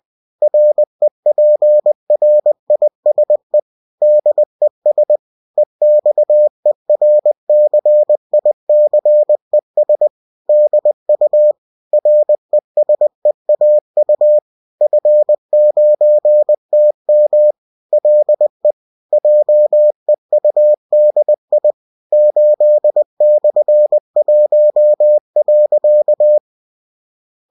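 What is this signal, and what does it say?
Morse code, heard as a keyed tone, about 20 wpm, 600 Hz. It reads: REPRISE DES EXERCICES DU RESEAU F9TM LE JEUDI 8/1.
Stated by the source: Bulletin sent in CW.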